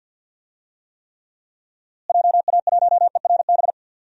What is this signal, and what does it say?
Morse code, heard as a keyed tone, about 49 wpm, 700 Hz. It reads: JA1ERB
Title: DXCC-100 Call Signs - 6 Characters - Repeated 3x - 50wpm